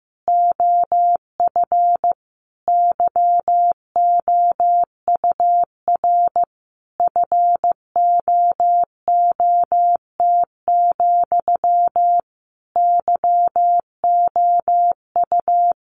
Morse code, heard as a keyed tone, about 15 wpm, 700 Hz. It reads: OF YOUR FOOT, YOU
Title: A Sound Of Thunder 15wpm